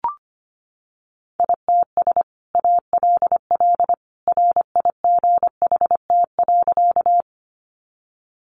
Morse code, heard as a keyed tone, about 25 wpm, 700 Hz. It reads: ITS ALL RIGHT.
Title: Sentences from Top 600 Words 25wpm